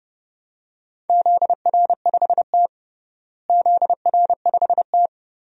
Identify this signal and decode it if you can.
Morse code, heard as a keyed tone, about 30 wpm, 700 Hz. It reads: ZR5T ZR5T
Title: DXCC-100 Call Signs - 4 Characters - Repeated 2x - 30wpm